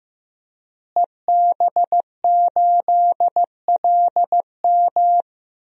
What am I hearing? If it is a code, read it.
Morse code, heard as a keyed tone, about 15 wpm, 700 Hz. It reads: EB8LM
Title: DXCC-100 Call Signs - 5 Characters - Repeated 3x - 15wpm